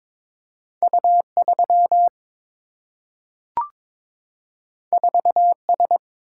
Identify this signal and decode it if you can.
Morse code, heard as a keyed tone, about 22 wpm, 700 Hz. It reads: U3 4S
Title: Single-Letter Single-Number 22wpm